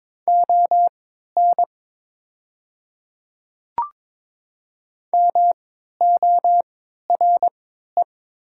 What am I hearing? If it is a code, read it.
Morse code, heard as a keyed tone, about 22 wpm, spaced as for 12 wpm, 700 Hz. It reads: ON MORE
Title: 3x Character Spacing - Top 100 Words - 22wpm